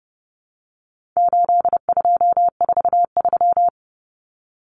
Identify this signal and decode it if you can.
Morse code, heard as a keyed tone, about 30 wpm, 700 Hz. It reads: 8243
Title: Common Cut Numbers - 4-Digit 30wpm